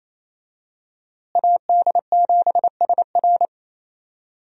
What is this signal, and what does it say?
Morse code, heard as a keyed tone, about 27 wpm, 700 Hz. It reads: AD7SR
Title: USA Call Signs - 5 Characters - Repeated 3x - 28wpm